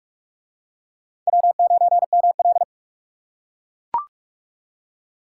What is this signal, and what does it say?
Morse code, heard as a keyed tone, about 44 wpm, 700 Hz. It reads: W9ML